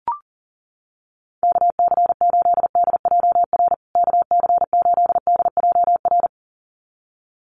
Morse code, heard as a keyed tone, about 40 wpm, 700 Hz. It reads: KC8DJR KC8DJR
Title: USA Call Signs Repeated 2x 40wpm